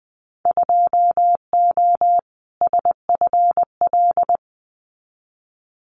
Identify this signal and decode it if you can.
Morse code, heard as a keyed tone, about 20 wpm, 700 Hz. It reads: 2O SFL
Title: Winter Field Day Exchanges 20wpm